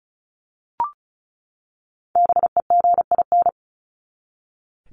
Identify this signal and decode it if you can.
Morse code, heard as a keyed tone, about 34 wpm, 700 Hz. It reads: BEGIN
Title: Top 500 Words 35wpm